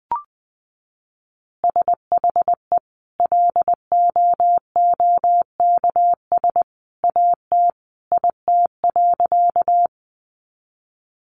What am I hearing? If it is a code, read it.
Morse code, heard as a keyed tone, about 20 wpm, 700 Hz. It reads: SHE LOOKS AT IT.
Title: Additional Easy Phrases 20wpm